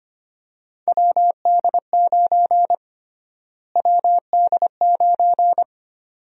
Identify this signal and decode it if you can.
Morse code, heard as a keyed tone, about 25 wpm, 700 Hz. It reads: WD9 WD9
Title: USA Call Sign Prefixes - Repeated 2x - 25wpm